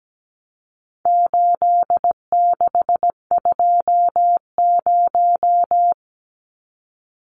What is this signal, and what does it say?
Morse code, heard as a keyed tone, about 17 wpm, 700 Hz. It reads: 8620
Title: Numbers - 4-Digit 17wpm